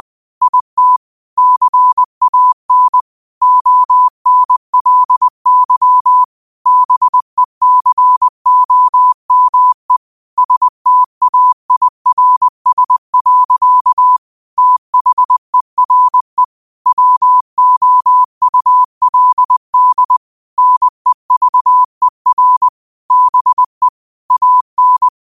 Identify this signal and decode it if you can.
Morse code, heard as a keyed tone, about 20 wpm, 1000 Hz. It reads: IT CAN ONLY BECOME STAIRS. THERE WOULD NEVER BE AN
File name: qotdfiles / 20wpm